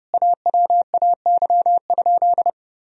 Morse code, heard as a keyed tone, about 30 wpm, 700 Hz. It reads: AWAY?
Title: Sentences from Top 1000 Words - Review Part 2 - 30wpm